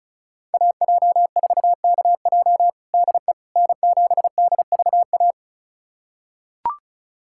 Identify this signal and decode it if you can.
Morse code, heard as a keyed tone, about 36 wpm, 700 Hz. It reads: AJ4KJ DE N7DVA
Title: USA Callsign DE Callsign - 5 Characters - 35wpm